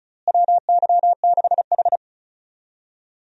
Morse code, heard as a keyed tone, about 36 wpm, 700 Hz. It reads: WY6H
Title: USA Call Signs - 4 Characters - Repeated 3x - 35wpm